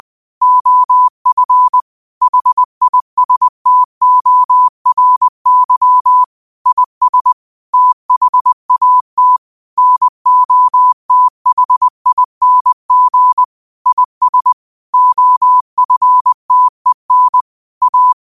Morse code, heard as a keyed tone, about 20 wpm, 1000 Hz. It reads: OF HISTORY IS THAT NOTHING IS OFTEN A
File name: qotdfiles / 20wpm